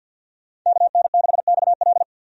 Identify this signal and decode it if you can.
Morse code, heard as a keyed tone, about 52 wpm, 700 Hz. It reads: KN6XL
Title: USA Call Signs - 5 Characters - 50wpm